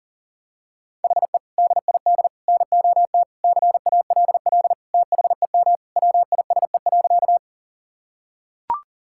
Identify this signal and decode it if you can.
Morse code, heard as a keyed tone, about 40 wpm, 700 Hz. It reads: HE DID NOT CALL THEM WISE.
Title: Sentences from Top 1600 Words 40wpm